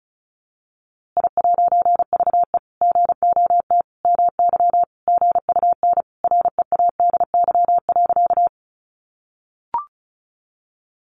Morse code, heard as a keyed tone, about 34 wpm, 700 Hz. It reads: I'VE GOT MY GUN READY.
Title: A Sound Of Thunder 35wpm